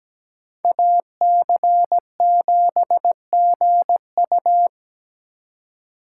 Morse code, heard as a keyed tone, about 20 wpm, 700 Hz. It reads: AC7GU
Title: Speed Racing - USA Call Signs 17wpm